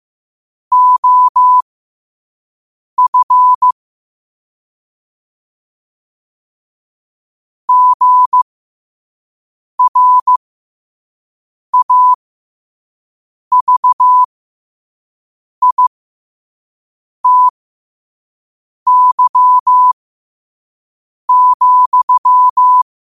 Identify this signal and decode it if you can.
Morse code, heard as a keyed tone, about 15 wpm, spaced as for 5 wpm, 1000 Hz. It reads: OF GRAVITY,
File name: qotdfiles / 5wpm